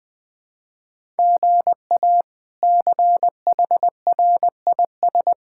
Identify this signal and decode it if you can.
Morse code, heard as a keyed tone, about 20 wpm, 700 Hz. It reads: GA CHRIS
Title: K1USN SST QSO Practice - 20wpm